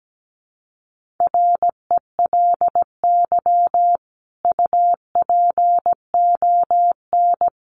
Morse code, heard as a keyed tone, about 17 wpm, 700 Hz. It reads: RELY UPON
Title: Sentences from Top 1200 Words 17wpm